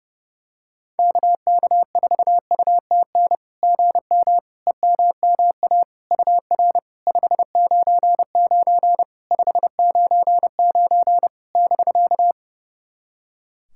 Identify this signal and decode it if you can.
Morse code, heard as a keyed tone, about 30 wpm, 660 Hz. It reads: KK4UTN GM EMMA UR 599 599 <BK>
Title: SOTA QSO Practice 30wpm